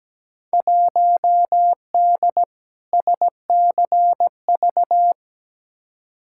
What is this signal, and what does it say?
Morse code, heard as a keyed tone, about 17 wpm, 700 Hz. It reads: 1D SCV
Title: Field Day Exchanges Encore - 17wpm